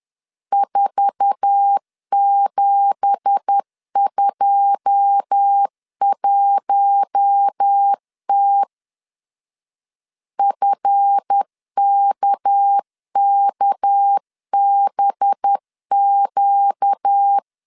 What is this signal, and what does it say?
Morse code, heard as a keyed tone, about 10 wpm, 800 Hz. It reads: 4721T FKKBQ
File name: CW / DET / 1 9 1500C S C 10wpm 4557